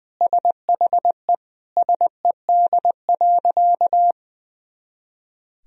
Morse code, heard as a keyed tone, about 20 wpm, 700 Hz. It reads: SHE SED.
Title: Sentences from Top 400 Words and Abbreviations 20wpm